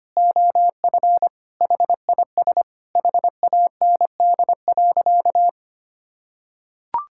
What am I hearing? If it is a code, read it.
Morse code, heard as a keyed tone, about 25 wpm, 700 Hz. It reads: OF HIS HAND.
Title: Sentences from Top 1000 Words - Review Part 6 - 25wpm